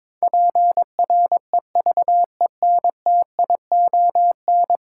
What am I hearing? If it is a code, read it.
Morse code, heard as a keyed tone, about 22 wpm, 700 Hz. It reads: PREVENTION
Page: morse-code-ninja-practice-22wpm